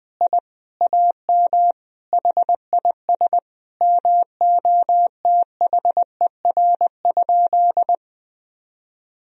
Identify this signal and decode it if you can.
Morse code, heard as a keyed tone, about 20 wpm, 700 Hz. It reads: I AM HIS MOTHER?
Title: Sentences from Top 1000 Words - Review Part 2 - 20wpm